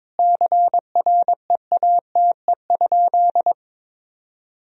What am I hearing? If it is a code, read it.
Morse code, heard as a keyed tone, about 22 wpm, 700 Hz. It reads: CREATE?
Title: Sentences from Top 1000 Words - Review Part 3 - 22wpm